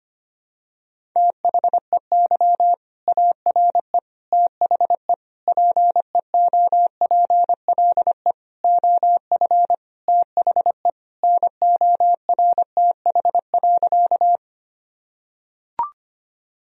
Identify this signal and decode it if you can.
Morse code, heard as a keyed tone, about 25 wpm, 700 Hz. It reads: THEY ARE THE PEOPLE OF THE NORTH.
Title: Sentences from Top 1000 Words - Review Part 1 - 25wpm